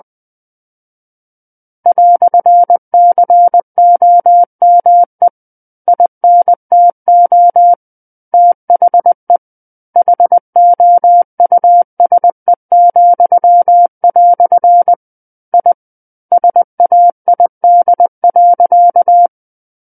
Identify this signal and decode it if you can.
Morse code, heard as a keyed tone, about 20 wpm, 700 Hz. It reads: "COME INTO THE HOUSE," I SAID.
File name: 20 WPM